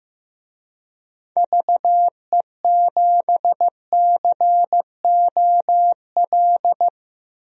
Morse code, heard as a keyed tone, about 15 wpm, 700 Hz. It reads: VE7COL
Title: DXCC-100 Call Signs - 6 Characters - 15wpm